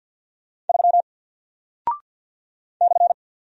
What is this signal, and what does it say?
Morse code, heard as a keyed tone, about 48 wpm, 700 Hz. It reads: 3 /
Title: Single Character (Letter:Number:Punctuation:Prosign) Alt 50wpm